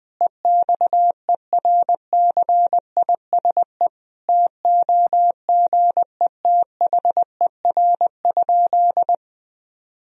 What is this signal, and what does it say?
Morse code, heard as a keyed tone, about 20 wpm, 700 Hz. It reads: EXERCISE TOGETHER?
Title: Sentences from Top 1000 Words - Review Part 4 - 20wpm